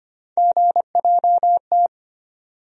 Morse code, heard as a keyed tone, about 25 wpm, 700 Hz. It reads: GJT
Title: USA Call Sign Postfixes - 3 Letters 25wpm